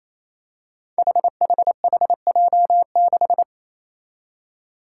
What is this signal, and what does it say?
Morse code, heard as a keyed tone, about 28 wpm, 700 Hz. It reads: HHHJ6
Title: Five Challenging Letter-Number Combinations 28wpm